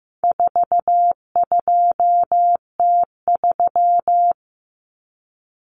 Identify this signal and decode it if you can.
Morse code, heard as a keyed tone, about 15 wpm, 700 Hz. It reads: 42T3
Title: Common Cut Numbers - 4-Digit 15wpm